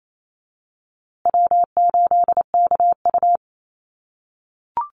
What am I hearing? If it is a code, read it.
Morse code, heard as a keyed tone, about 28 wpm, 700 Hz. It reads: W8KU
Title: USA Call Signs Repeated 2x 28wpm